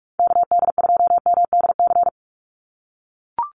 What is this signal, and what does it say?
Morse code, heard as a keyed tone, about 44 wpm, 700 Hz. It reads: KD2MDC
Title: USA Call Signs 45wpm